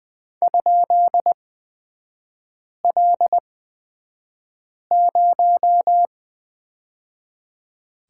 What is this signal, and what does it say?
Morse code, heard as a keyed tone, about 20 wpm, spaced as for 5 wpm, 700 Hz. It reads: ?L0
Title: 236 Lesson 38e - Letter q - Three Characters 20wpm